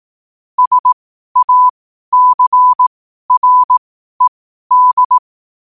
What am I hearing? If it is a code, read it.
Morse code, heard as a keyed tone, about 18 wpm, spaced as for 12 wpm, 1000 Hz. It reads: SACRED
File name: qotdfiles / 12wpm